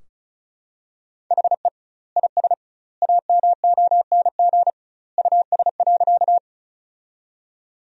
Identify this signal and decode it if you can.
Morse code, heard as a keyed tone, about 34 wpm, 700 Hz. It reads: HE IS AMONG US.
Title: 2x Word-Spacing - Sentences from Top 500 Words - 35wpm